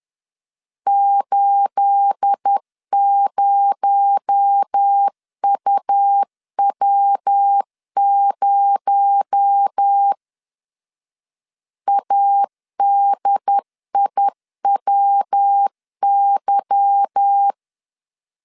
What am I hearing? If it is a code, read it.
Morse code, heard as a keyed tone, about 10 wpm, 800 Hz. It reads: 80UW0 ADIWY
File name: CW / DET / 1 9 1500C S A 10wpm 4511